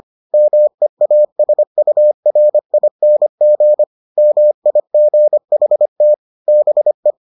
Morse code, heard as a keyed tone, about 25 wpm, 600 Hz. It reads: MEASURING MIGHT BE